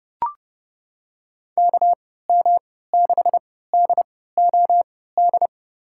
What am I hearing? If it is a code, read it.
Morse code, heard as a keyed tone, about 30 wpm, spaced as for 17 wpm, 700 Hz. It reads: KM6DOD
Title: USA Call Signs - 6 Characters - 3x Character Spacing - 30wpm